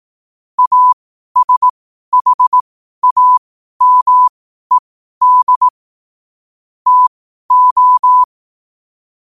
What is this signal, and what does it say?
Morse code, heard as a keyed tone, about 18 wpm, spaced as for 12 wpm, 1000 Hz. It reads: ASHAMED TO
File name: qotdfiles / 12wpm